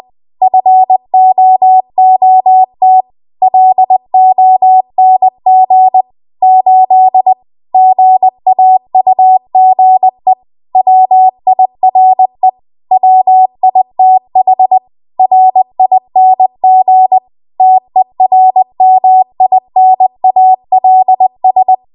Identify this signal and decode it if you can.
Morse code, heard as a keyed tone, about 20 wpm, 750 Hz. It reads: FOOT LONG 8 GAUGE WIRE WITH RING TERMINALS
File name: Morse / 20WPM